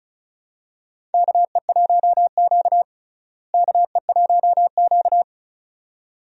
Morse code, heard as a keyed tone, about 36 wpm, 700 Hz. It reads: KE1Q KE1Q
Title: USA Call Signs - 4 Characters - Repeated 2x - 35wpm